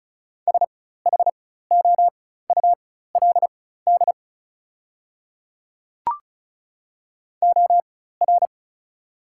Morse code, heard as a keyed tone, about 35 wpm, spaced as for 16 wpm, 700 Hz. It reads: SHOULD OR
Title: 4x Character Spacing - Top 300 Words - 35wpm